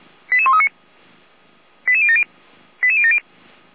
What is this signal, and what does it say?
Five-tone selective call (ZVEI), heard as quick runs of five tones, tones of about 70 ms. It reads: 88139 88070 88070